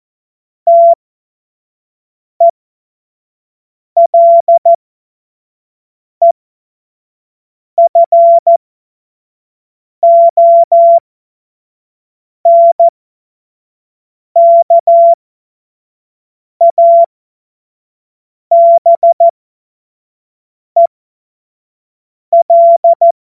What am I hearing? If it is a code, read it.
Morse code, heard as a keyed tone, about 14 wpm, spaced as for 5 wpm, 680 Hz. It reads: TELEFONKABEL
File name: Morsekurs / Witze / Witz 20 Februar 2024 25 F70